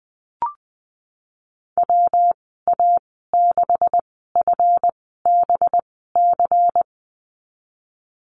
Morse code, heard as a keyed tone, about 20 wpm, spaced as for 15 wpm, 700 Hz. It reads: WA6FBC
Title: 200 Lesson 32g - Letter k - Call Signs 20wpm